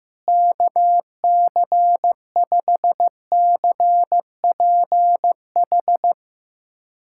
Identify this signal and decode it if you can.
Morse code, heard as a keyed tone, about 15 wpm, 700 Hz. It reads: KC5CPH
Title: USA Callsign DE Callsign - 6 Characters - 15wpm